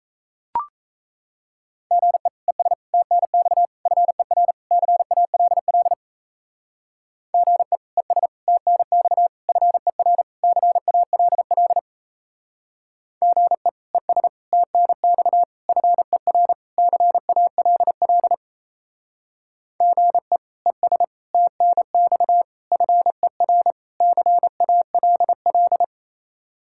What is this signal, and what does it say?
Morse code, heard as a keyed tone, about 34 wpm, 700 Hz. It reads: GE ES TNX FER CALL GE ES TNX FER CALL GE ES TNX FER CALL GE ES TNX FER CALL
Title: Speed-Racing Common QSO Phrases Encore - 28wpm